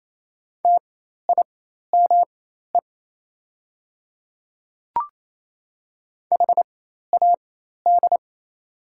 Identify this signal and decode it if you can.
Morse code, heard as a keyed tone, about 27 wpm, spaced as for 13 wpm, 690 Hz. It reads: TIME HAD
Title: 4x Character Spacing - Top 100 Words - 28wpm